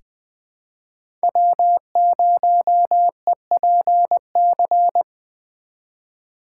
Morse code, heard as a keyed tone, about 20 wpm, 700 Hz. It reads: W0EPC
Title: K1USN SST QSO Practice - 20wpm